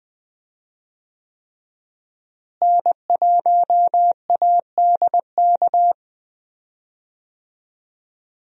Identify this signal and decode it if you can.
Morse code, heard as a keyed tone, about 21 wpm, 700 Hz. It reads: N1ADK
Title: USA Call Signs - 5 Characters - Repeated 3x - 20wpm